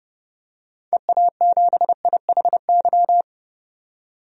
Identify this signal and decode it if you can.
Morse code, heard as a keyed tone, about 30 wpm, 700 Hz. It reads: EA7IHY
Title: DXCC-100 Call Signs - 6 Characters - 30wpm